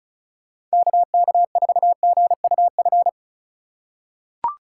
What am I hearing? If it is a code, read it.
Morse code, heard as a keyed tone, about 34 wpm, 700 Hz. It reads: KK4GUF
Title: USA Call Signs - 3x Character Spacing - 35wpm